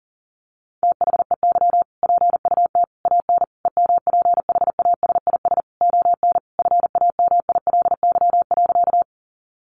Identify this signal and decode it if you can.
Morse code, heard as a keyed tone, about 40 wpm, 700 Hz. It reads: THEY PUT AN EMPHASIS ON FAMILY.
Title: Sentences from Top 1400 Words 40wpm